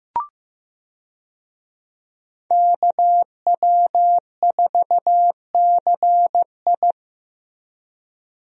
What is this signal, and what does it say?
Morse code, heard as a keyed tone, about 15 wpm, 700 Hz. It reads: KW4CI
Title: USA Call Signs - 5 Characters - Repeated 2x - 15wpm